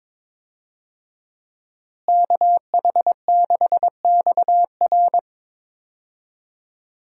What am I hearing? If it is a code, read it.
Morse code, heard as a keyed tone, about 22 wpm, 700 Hz. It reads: KH6XR
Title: DXCC-100 Call Signs - 5 Characters - Repeated 3x - 22wpm